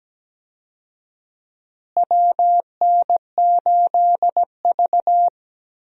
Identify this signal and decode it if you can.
Morse code, heard as a keyed tone, about 17 wpm, 700 Hz. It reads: WN8V
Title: USA Call Signs - 4 Characters - Repeated 2x - 17wpm